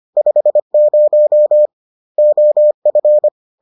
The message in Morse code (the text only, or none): 50 OF